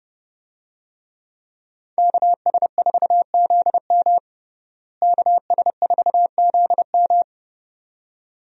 Morse code, heard as a keyed tone, about 29 wpm, 700 Hz. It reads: KS4ZM KS4ZM
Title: USA Call Signs - 5 Characters - Repeated 2x - 30wpm